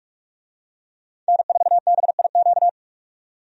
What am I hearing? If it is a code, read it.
Morse code, heard as a keyed tone, about 47 wpm, 700 Hz. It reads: N4BIQ
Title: USA Call Signs - 5 Characters - 45wpm